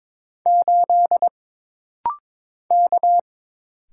Morse code, heard as a keyed tone, about 22 wpm, 700 Hz. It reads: 8 K
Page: This rapid-fire timing is geared to push your ICR (Instant Character Recognition) skills.